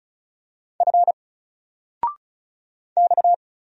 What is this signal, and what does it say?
Morse code, heard as a keyed tone, about 36 wpm, 700 Hz. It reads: F X
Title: Single Letter-Number 35wpm